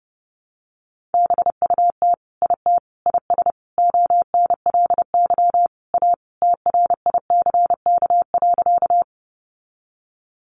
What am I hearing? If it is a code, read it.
Morse code, heard as a keyed tone, about 30 wpm, 700 Hz. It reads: BUT IT IS ONLY A TRICK.
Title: Sentences from Top 1500 Words 30wpm